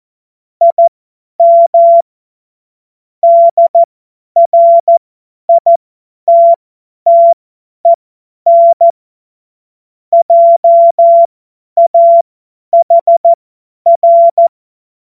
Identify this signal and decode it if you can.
Morse code, heard as a keyed tone, about 14 wpm, spaced as for 10 wpm, 680 Hz. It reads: IM DRITTEN JAHR
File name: Morsekurs / Witze / Witz 05 März 2024 50 F70